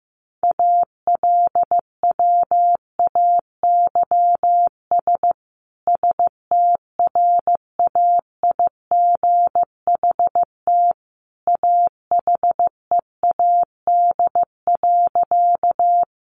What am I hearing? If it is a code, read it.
Morse code, heard as a keyed tone, about 15 wpm, 700 Hz. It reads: ALWAYS STRAIGHT AHEAD.
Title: Sentences from Top 1100 Words 15wpm